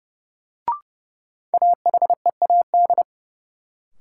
Morse code, heard as a keyed tone, about 30 wpm, 700 Hz. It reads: AHEAD
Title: Sentences from Top 1100 Words - New Word Drill Alt - 30wpm